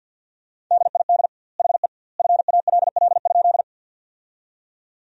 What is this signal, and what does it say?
Morse code, heard as a keyed tone, about 50 wpm, 700 Hz. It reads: DID HE FALL?
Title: 2x Word-Spacing - Sentences from Top 500 Words - 50wpm